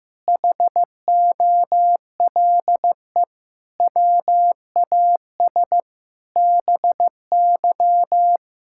Morse code, heard as a keyed tone, about 16 wpm, 700 Hz. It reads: HOLE WAS BY
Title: Sentences from Top 1600 Words 15wpm